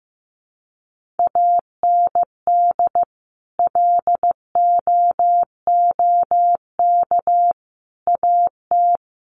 Morse code, heard as a keyed tone, about 15 wpm, 700 Hz. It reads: AND LOOK AT
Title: Sentences from Top 1500 Words 15wpm